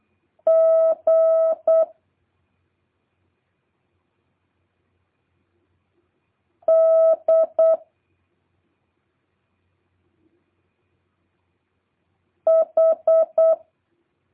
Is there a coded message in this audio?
Yes: Morse code from 0.5 to 13.5 s, heard as a keyed tone, about 8 wpm, 1300 Hz. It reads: G D H